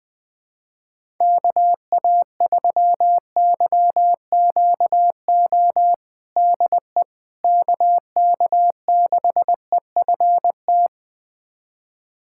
Morse code, heard as a keyed tone, about 20 wpm, 700 Hz. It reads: KA3YQO DE KK6EFT
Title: USA Callsign DE Callsign - 6 Characters - 20wpm